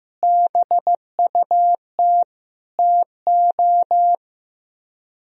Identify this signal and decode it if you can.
Morse code, heard as a keyed tone, about 16 wpm, 700 Hz. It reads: BUT TO
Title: Speed-Racing - Sets of 5 Words 15wpm